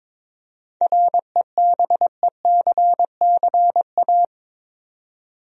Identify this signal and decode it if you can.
Morse code, heard as a keyed tone, about 21 wpm, 700 Hz. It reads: REBECCA
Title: USA Common Names - 22wpm